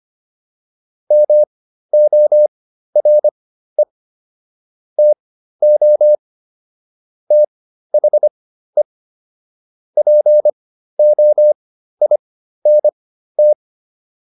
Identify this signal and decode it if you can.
Morse code, heard as a keyed tone, about 25 wpm, spaced as for 13 wpm, 600 Hz. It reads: MORE TO THE POINT
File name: PR402 13